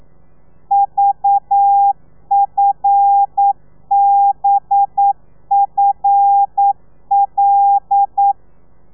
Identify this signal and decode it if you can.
Morse code, heard as a keyed tone, about 9 wpm, 800 Hz. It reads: VFBFL
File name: telegrafia / i2tfj / Lez5 40